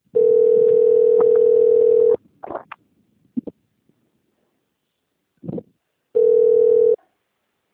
Ringback tone (North America), 440 + 480 Hz.
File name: out- - - - - .55755